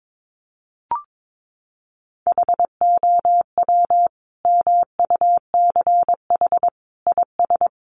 Morse code, heard as a keyed tone, about 22 wpm, 700 Hz. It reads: HOW MUCH IS
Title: Sentences from Top 1100 Words 22wpm